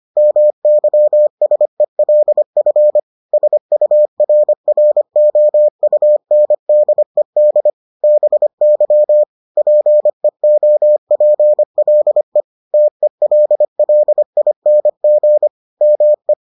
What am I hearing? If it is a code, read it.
Morse code, heard as a keyed tone, about 25 wpm, 600 Hz. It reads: MYSELF SURROUNDED BY PEOPLE TELLING ME